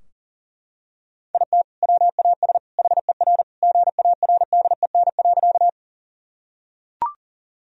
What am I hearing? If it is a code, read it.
Morse code, heard as a keyed tone, about 40 wpm, 700 Hz. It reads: IT WAS HER GARDEN.
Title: Sentences from Top 1000 Words - Review Part 4 - 40wpm